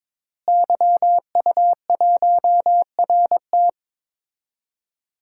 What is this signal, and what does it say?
Morse code, heard as a keyed tone, about 21 wpm, 700 Hz. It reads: YU1RT
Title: DXCC-100 Call Signs - 5 Characters - Repeated 3x - 22wpm